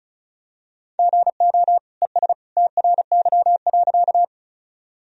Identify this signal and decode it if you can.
Morse code, heard as a keyed tone, about 36 wpm, 700 Hz. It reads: GO ES TRY.
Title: Sentences from Top 200 Words and Abbreviations 35wpm